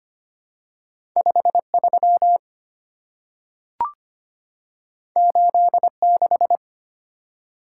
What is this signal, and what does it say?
Morse code, heard as a keyed tone, about 25 wpm, 700 Hz. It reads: 53 86
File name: Two Letter-Number Combinations - 25wpm